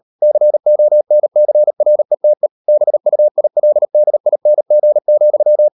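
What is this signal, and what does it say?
Morse code, heard as a keyed tone, about 38 wpm, 600 Hz. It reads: CONCRETE BUILDING,
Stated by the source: Morse code practice at 38 WPM, sending Ken Follett's novel Never in English, from CW med Gnister.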